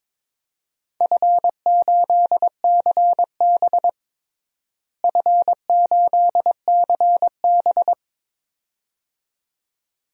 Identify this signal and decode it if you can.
Morse code, heard as a keyed tone, about 22 wpm, 700 Hz. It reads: F8CB F8CB
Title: DXCC-100 Call Signs - 4 Characters - Repeated 2x - 22wpm